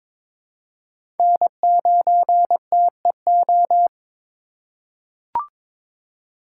Morse code, heard as a keyed tone, about 22 wpm, 700 Hz. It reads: N9TEO
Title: USA Call Signs - 2x Character Spacing - 22wpm